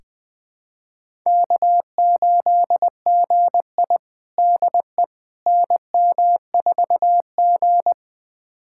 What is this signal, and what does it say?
Morse code, heard as a keyed tone, about 21 wpm, 700 Hz. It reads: K8GI DE NM4G
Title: USA Callsign DE Callsign - 4 Characters - 20wpm